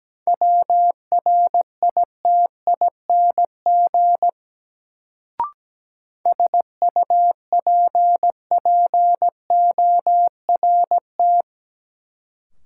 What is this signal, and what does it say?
Morse code, heard as a keyed tone, about 17 wpm, 700 Hz. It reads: WRITING SUPPORT
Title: Sentences from Top 800 Words - New Word Drill - 17wpm